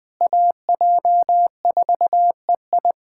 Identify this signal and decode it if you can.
Morse code, heard as a keyed tone, about 20 wpm, 700 Hz. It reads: AJ4EI
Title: USA Callsign DE Callsign - 5 Characters - 20wpm